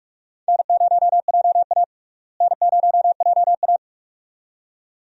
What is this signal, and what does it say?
Morse code, heard as a keyed tone, about 41 wpm, 700 Hz. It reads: N0JA N0JA
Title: USA Call Signs - 4 Characters - Repeated 2x - 45wpm